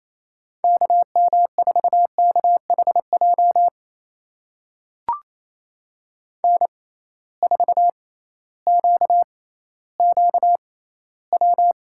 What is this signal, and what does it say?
Morse code, heard as a keyed tone, about 28 wpm, spaced as for 15 wpm, 700 Hz. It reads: KM4KHJ N4QQW
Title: USA Call Signs - 6x Character Spacing - 28wpm